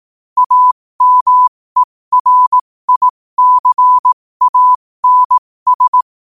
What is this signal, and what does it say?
Morse code, heard as a keyed tone, about 18 wpm, spaced as for 15 wpm, 1000 Hz. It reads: AMERICANS
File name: qotdfiles / 15wpm